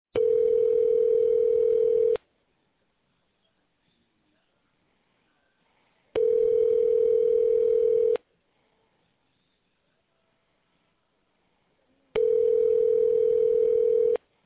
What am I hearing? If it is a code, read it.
Ringback tone (North America), 440 + 480 Hz.